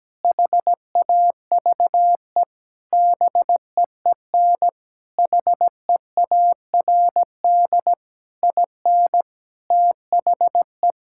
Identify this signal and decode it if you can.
Morse code, heard as a keyed tone, about 17 wpm, 700 Hz. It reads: HAVE BEEN HEARD IN THE
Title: Sentences from Top 1000 Words - Review Part 5 - 17wpm